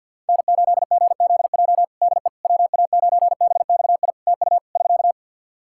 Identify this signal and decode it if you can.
Morse code, heard as a keyed tone, about 48 wpm, 700 Hz. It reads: N8GZJ DE WA9BXI TU <SK>
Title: K1USN SST QSO Practice 50wpm